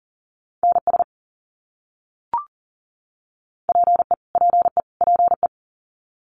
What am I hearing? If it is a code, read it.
Morse code, heard as a keyed tone, about 40 wpm, 700 Hz. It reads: NS PE PE PE
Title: Canadian Provinces and Territories - Repeated 3x - 40wpm